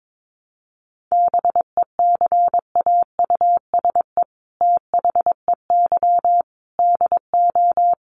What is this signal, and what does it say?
Morse code, heard as a keyed tone, about 22 wpm, 700 Hz. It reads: BECAUSE THEY DO
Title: Sets of 5 Words Encore2 22wpm